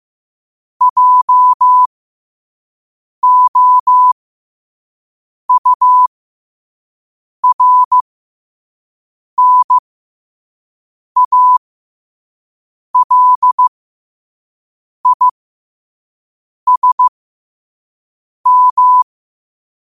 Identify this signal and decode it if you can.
Morse code, heard as a keyed tone, about 15 wpm, spaced as for 5 wpm, 1000 Hz. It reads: JOURNALISM
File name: qotdfiles / 5wpm